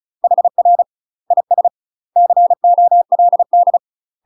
Morse code, heard as a keyed tone, about 34 wpm, 700 Hz. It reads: HR IS COLD
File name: UK MASTER QSOs TRAINING / 35WPM / UK MASTER QSOs 0004